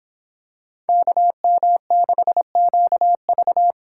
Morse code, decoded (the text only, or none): KM6QV